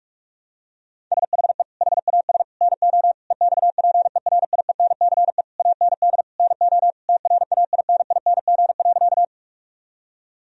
Morse code, heard as a keyed tone, about 47 wpm, 700 Hz. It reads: SHE HAS NO EXPERIENCE AND NO TRAINING.
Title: Sentences from Top 1000 Words - Review Part 4 - 45wpm